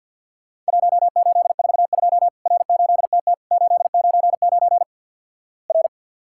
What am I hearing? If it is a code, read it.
Morse code, heard as a keyed tone, about 50 wpm, 700 Hz. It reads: 1842 R8TT 899 R
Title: Worked All Europe DX Contest - QTC Exchange - 50wpm